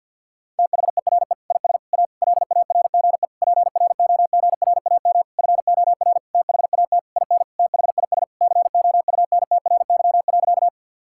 Morse code, heard as a keyed tone, about 46 wpm, 700 Hz. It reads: THERE IS A LARGE PROGRAM FOR THAT IN THIS COUNTRY.
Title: Sentences from Top 1000 Words - Review Part 5 - 50wpm